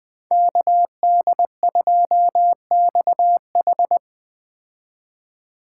Morse code, heard as a keyed tone, about 21 wpm, 700 Hz. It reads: KD2XH